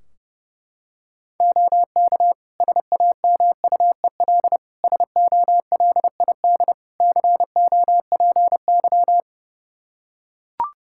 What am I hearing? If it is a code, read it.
Morse code, heard as a keyed tone, about 32 wpm, 700 Hz. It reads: OK SAMUEL SOLID COPY
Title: K1USN SST QSO Practice 30wpm